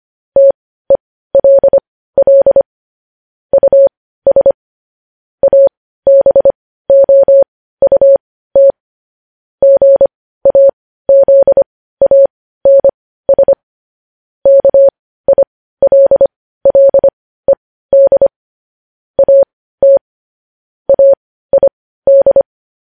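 Morse code, heard as a keyed tone, about 25 wpm, spaced as for 15 wpm, 550 Hz. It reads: TELL US ABOUT GAZANS KILLED AT AID
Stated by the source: Morse code news-headline practice sent at 15 words per minute.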